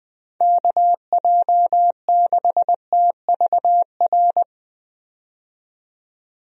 Morse code, heard as a keyed tone, about 20 wpm, 700 Hz. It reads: KJ6TVR
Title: USA Call Signs - 6 Characters - 20wpm